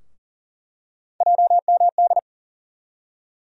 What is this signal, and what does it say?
Morse code, heard as a keyed tone, about 40 wpm, 700 Hz. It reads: JMD
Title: USA Call Sign Postfixes - 3 Letters 40wpm